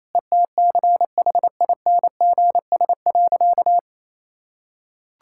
Morse code, heard as a keyed tone, about 28 wpm, 700 Hz. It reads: ETCHINGS.
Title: Pangram Challenge v2 - 28wpm